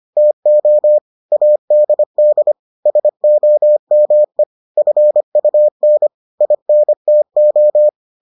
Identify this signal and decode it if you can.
Morse code, heard as a keyed tone, about 25 wpm, 600 Hz. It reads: TO ADD SOME FUN INTO